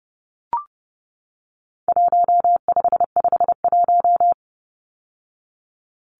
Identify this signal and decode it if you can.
Morse code, heard as a keyed tone, about 30 wpm, 700 Hz. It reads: 1551
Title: Numbers - Contest Style - 4-Digit 30wpm